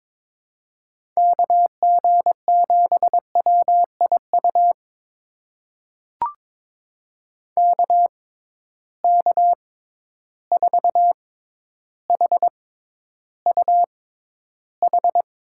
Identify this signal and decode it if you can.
Morse code, heard as a keyed tone, about 21 wpm, spaced as for 11 wpm, 700 Hz. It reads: KG7WIU KK4HUH
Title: USA Call Signs - 6 Characters - 6x Character Spacing - 22wpm